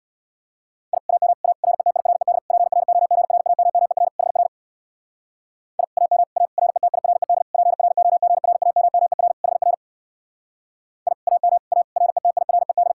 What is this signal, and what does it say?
Morse code, heard as a keyed tone, about 64 wpm, 700 Hz. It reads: I AM A RETIRED PROGRAMMER HR I AM A RETIRED PROGRAMMER HR I AM A RETIRED